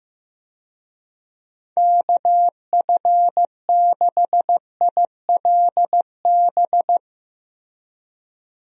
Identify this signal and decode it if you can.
Morse code, heard as a keyed tone, about 15 wpm, 700 Hz. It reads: KF6ILB
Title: USA Call Signs - 6 Characters - Repeated 2x - 15wpm